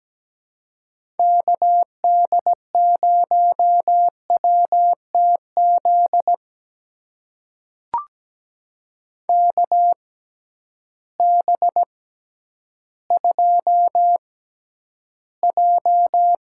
Morse code, heard as a keyed tone, about 17 wpm, spaced as for 10 wpm, 700 Hz. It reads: KD0WTZ KB2J
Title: USA Call Signs - 6 Characters - 6x Character Spacing - 17wpm